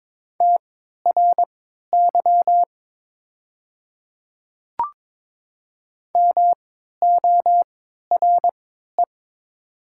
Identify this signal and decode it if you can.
Morse code, heard as a keyed tone, about 22 wpm, spaced as for 12 wpm, 700 Hz. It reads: TRY MORE
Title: 3x Character Spacing - Top 200 Words - 22wpm